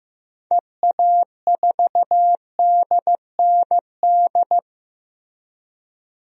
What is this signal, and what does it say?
Morse code, heard as a keyed tone, about 15 wpm, 700 Hz. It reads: EA4DND
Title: DXCC-100 Call Signs - 6 Characters - 15wpm